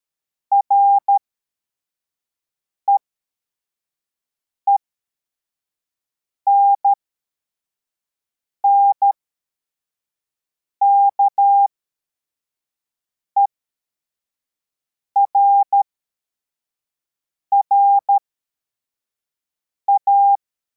Morse code, heard as a keyed tone, about 13 wpm, spaced as for 4 wpm, 800 Hz. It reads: REENN KERRA